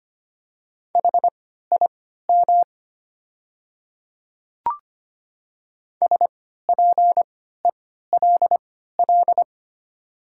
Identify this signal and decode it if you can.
Morse code, heard as a keyed tone, about 25 wpm, spaced as for 14 wpm, 700 Hz. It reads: HIM SPELL
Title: 3x Character Spacing - Top 300 Words - 25wpm